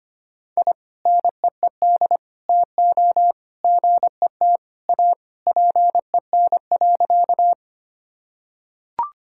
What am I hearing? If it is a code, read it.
Morse code, heard as a keyed tone, about 25 wpm, 700 Hz. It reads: I NEED TO GET A PEN.
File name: practice / 25wpm / Sentences with 2-Word Phrases - 25wpm